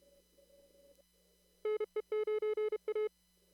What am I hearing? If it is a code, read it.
Morse code, heard as a keyed tone, about 32 wpm, 430 Hz. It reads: NE9A
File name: CW / kt5j wpx cw-op-k5pi